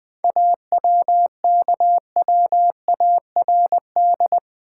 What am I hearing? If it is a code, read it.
Morse code, heard as a keyed tone, about 20 wpm, 690 Hz. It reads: AWKWARD
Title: 7 Letter Words - First 3 Letters Given - 20wpm